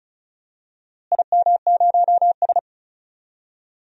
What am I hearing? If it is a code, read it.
Morse code, heard as a keyed tone, about 34 wpm, 700 Hz. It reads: IM0S
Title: DXCC-100 Call Signs - 4 Characters - Repeated 2x - 35wpm